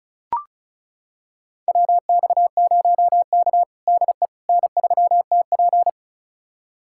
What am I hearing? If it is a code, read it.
Morse code, heard as a keyed tone, about 34 wpm, 700 Hz. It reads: WX0K DE N3TP
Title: USA Callsign DE Callsign - 4 Characters - 35wpm